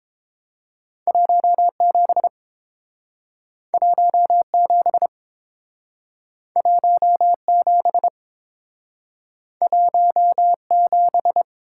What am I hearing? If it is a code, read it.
Morse code, heard as a keyed tone, about 30 wpm, 700 Hz. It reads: 17 17 17 17